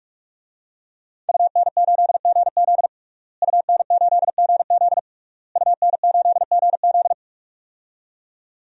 Morse code, heard as a keyed tone, about 43 wpm, 700 Hz. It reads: UN8GZ UN8GZ UN8GZ
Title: DXCC-100 Call Signs - 5 Characters - Repeated 3x - 45wpm